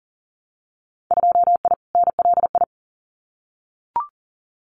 Morse code, heard as a keyed tone, about 40 wpm, 700 Hz. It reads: 2I NLI
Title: Winter Field Day Exchanges 40wpm